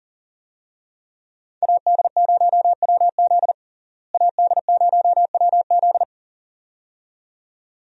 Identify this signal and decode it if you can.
Morse code, heard as a keyed tone, about 40 wpm, 700 Hz. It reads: AD0WZ AD0WZ
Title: USA Call Signs - 5 Characters - Repeated 2x - 40wpm